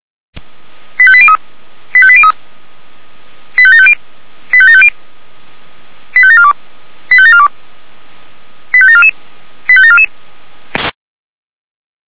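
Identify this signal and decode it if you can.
Five-tone selective call (ZVEI), heard as quick runs of five tones, tones of about 70 ms. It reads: 85803 85803 85750 85750 85742 85742 85744 85744
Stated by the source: ZVEI five-tone selective call received by SDR.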